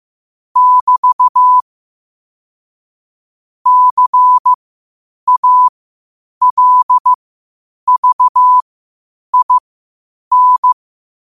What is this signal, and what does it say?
Morse code, heard as a keyed tone, about 15 wpm, spaced as for 8 wpm, 1000 Hz. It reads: <BT> CALVIN
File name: qotdfiles / 8wpm